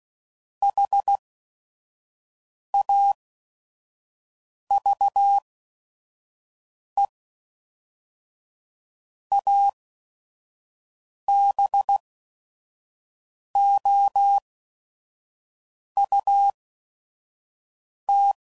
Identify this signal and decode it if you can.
Morse code, heard as a keyed tone, about 16 wpm, spaced as for 5 wpm, 800 Hz. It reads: HAVE ABOUT
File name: CW Code Groups @5WPM / QSOs / QSO WA7QOX 1